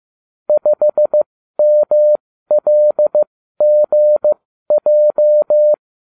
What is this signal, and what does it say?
Morse code, heard as a keyed tone, about 15 wpm, spaced as for 12 wpm, 600 Hz. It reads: 5MLGJ